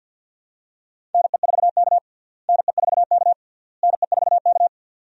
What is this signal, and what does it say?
Morse code, heard as a keyed tone, about 52 wpm, 700 Hz. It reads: NE4K NE4K NE4K
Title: USA Call Signs - 4 Characters - Repeated 3x - 50wpm